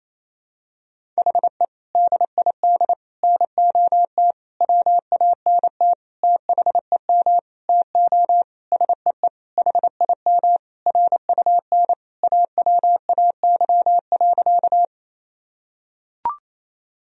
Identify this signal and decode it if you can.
Morse code, heard as a keyed tone, about 28 wpm, 700 Hz. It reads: HE DID NOT WANT THEM TO SEE HIM RUN AWAY.
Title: Sentences from Top 1000 Words - Review Part 1 - 28wpm